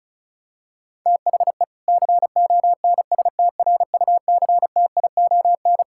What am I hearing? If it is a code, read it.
Morse code, heard as a keyed tone, about 36 wpm, 700 Hz. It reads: THE CONSTRUCTION
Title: Sentences from Top 1600 Words 35wpm